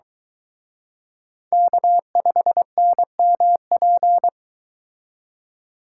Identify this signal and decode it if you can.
Morse code, heard as a keyed tone, about 23 wpm, 700 Hz. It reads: K5NMP